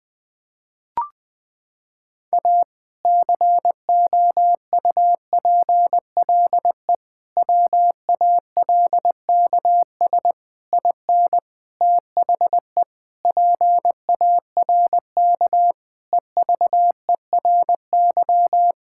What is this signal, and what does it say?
Morse code, heard as a keyed tone, about 21 wpm, 700 Hz. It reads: A COUPLE WALKS IN THE PARK EVERY
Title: Sentences with 2-Word Phrases - 20wpm